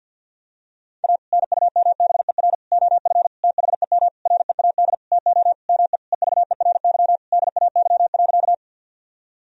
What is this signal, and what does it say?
Morse code, heard as a keyed tone, about 50 wpm, 700 Hz. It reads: A NUMBER OF THEM READ TO ME EVERY DAY.
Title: Sentences from Top 1000 Words - Review Part 1 - 50wpm